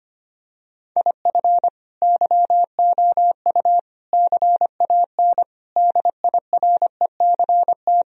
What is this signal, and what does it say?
Morse code, heard as a keyed tone, about 25 wpm, 700 Hz. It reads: IF YOU CAN DIRECT
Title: Sentences from Top 1000 Words - Review Part 2 - 25wpm